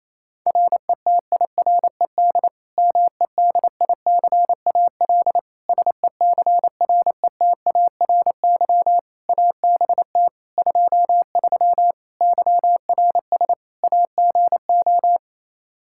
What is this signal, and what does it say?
Morse code, heard as a keyed tone, about 28 wpm, 700 Hz. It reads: RETIRED MEDICAL SECRETARY ABT 23 YRS AGO
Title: K1USN SST QSO Practice 28wpm